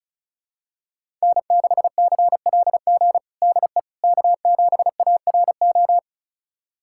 Morse code, heard as a keyed tone, about 36 wpm, 700 Hz. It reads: N6CLG DE K7ARO
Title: USA Callsign DE Callsign - 5 Characters - 35wpm